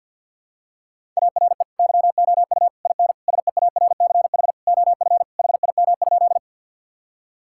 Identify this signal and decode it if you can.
Morse code, heard as a keyed tone, about 50 wpm, 700 Hz. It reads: ARE YOU IN SEARCH OF HIM?
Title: Sentences from Top 1000 Words - Review Part 5 - 50wpm